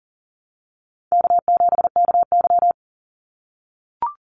Morse code, heard as a keyed tone, about 41 wpm, 700 Hz. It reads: K7KY
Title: Easy Call Signs - 40wpm